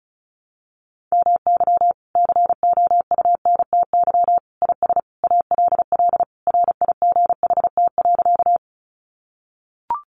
Morse code, heard as a keyed tone, about 34 wpm, 700 Hz. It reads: MY COUNTY IS ALL RIGHT.
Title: Sentences from Top 1200 Words 35wpm